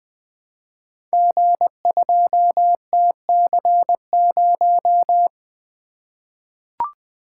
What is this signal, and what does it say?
Morse code, heard as a keyed tone, about 20 wpm, 700 Hz. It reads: G2TC0
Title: Code Groups - 5 Letter:Numbers - 20wpm